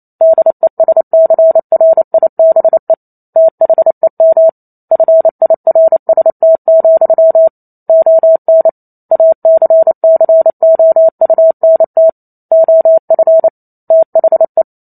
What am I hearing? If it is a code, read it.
Morse code, heard as a keyed tone, about 28 wpm, 650 Hz. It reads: DESCRIBE THEM FIRST, ON ACCOUNT OF THE